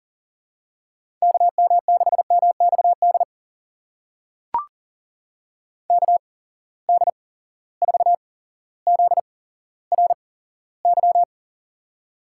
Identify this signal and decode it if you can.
Morse code, heard as a keyed tone, about 39 wpm, spaced as for 17 wpm, 700 Hz. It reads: KM6MXD KD4ZRY
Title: USA Call Signs - 6 Characters - 8x Character Spacing - 40wpm